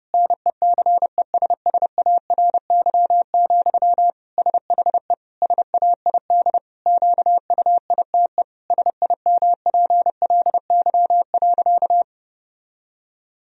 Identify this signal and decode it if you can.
Morse code, heard as a keyed tone, about 30 wpm, 700 Hz. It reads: NECESSARY, SHE SAID QUITE SIMPLY.
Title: Sentences from Top 1000 Words - Review Part 4 - 30wpm